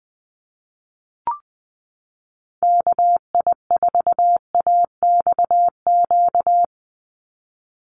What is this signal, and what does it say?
Morse code, heard as a keyed tone, about 20 wpm, 700 Hz. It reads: KI4AXQ
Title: USA Call Signs 20wpm